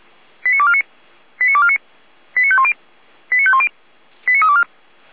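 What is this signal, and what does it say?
Five-tone selective call (ZVEI), heard as quick runs of five tones, tones of about 70 ms. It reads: 89249 89249 89510 89510 89324
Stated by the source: Five-tone ZVEI selective call received by SDR.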